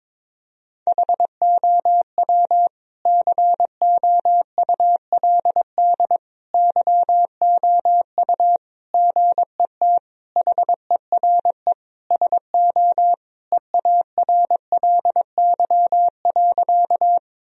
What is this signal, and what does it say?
Morse code, heard as a keyed tone, about 22 wpm, 700 Hz. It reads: HOW COULD YOU GET HERE SO EARLY.
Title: Sentences from Top 1000 Words - Review Part 2 - 22wpm